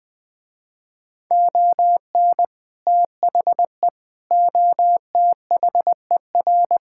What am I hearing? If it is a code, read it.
Morse code, heard as a keyed tone, about 20 wpm, 700 Hz. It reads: ON THE OTHER
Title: Sentences with 2-Word Phrases - Spoken Beginning - 20wpm